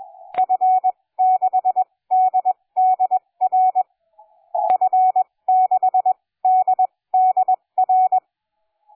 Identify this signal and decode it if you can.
Morse code, heard as a keyed tone, about 20 wpm, spaced as for 17 wpm, 740 Hz. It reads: F6DDR F6DDR
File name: cw forum / J79WR 1830 le 23-03-2009 05h46